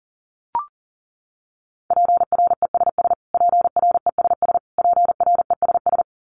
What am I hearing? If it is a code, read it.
Morse code, heard as a keyed tone, about 40 wpm, 700 Hz. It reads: PRESS PRESS PRESS
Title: Top 500 Words Repeated 40wpm